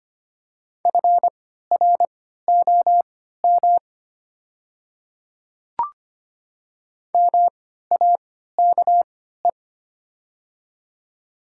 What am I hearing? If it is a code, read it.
Morse code, heard as a keyed tone, about 25 wpm, spaced as for 14 wpm, 700 Hz. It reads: FROM MAKE
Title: 3x Character Spacing - Top 100 Words - 25wpm